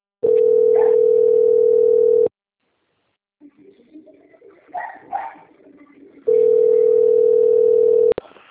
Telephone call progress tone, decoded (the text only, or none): ringback tone